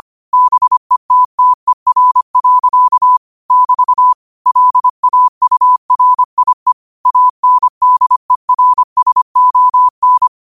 Morse code, heard as a keyed tone, about 25 wpm, 1000 Hz. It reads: BETTER. <BT> LAURIE ANDERSON
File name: qotdfiles / 25wpm